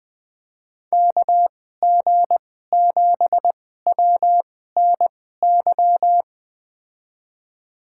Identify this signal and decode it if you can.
Morse code, heard as a keyed tone, about 21 wpm, spaced as for 15 wpm, 700 Hz. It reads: KG7WNY
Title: USA Call Signs - 6 Characters - 2x Character Spacing - 20wpm